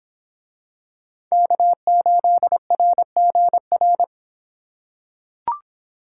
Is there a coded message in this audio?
Yes: Morse code from 1.3 to 4.0 s, heard as a keyed tone, about 26 wpm, 700 Hz. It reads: K8RGR